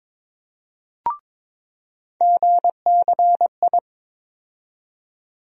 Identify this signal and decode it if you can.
Morse code, heard as a keyed tone, about 22 wpm, 700 Hz. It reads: GCI
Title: USA Call Sign Postfixes - 3 Letters 22wpm